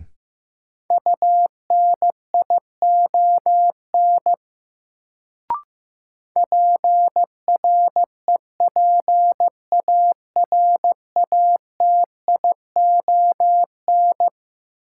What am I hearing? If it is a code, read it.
Morse code, heard as a keyed tone, about 15 wpm, 700 Hz. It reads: UNION PREPARATION
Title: Sentences from Top 1400 Words - New Word Drill - 15wpm